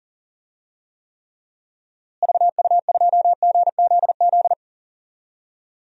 Morse code, heard as a keyed tone, about 40 wpm, 700 Hz. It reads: VU2GZZ
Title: DXCC-100 Call Signs - 6 Characters - Repeated 3x - 40wpm